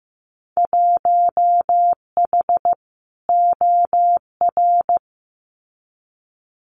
Morse code, heard as a keyed tone, about 15 wpm, 700 Hz. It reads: 1H OR
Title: Winter Field Day Exchanges 15wpm